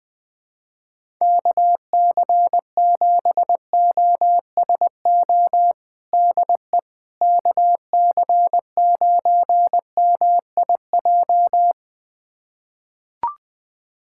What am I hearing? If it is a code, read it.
Morse code, heard as a keyed tone, about 20 wpm, 700 Hz. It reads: KC7OSO DE KC9MIJ
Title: USA Callsign DE Callsign - 6 Characters - 20wpm